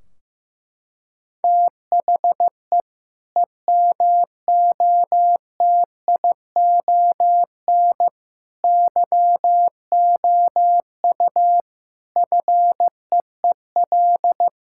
Morse code, heard as a keyed tone, about 15 wpm, 700 Hz. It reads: THE EMOTION YOU FEEL
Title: Sentences from Top 1600 Words 15wpm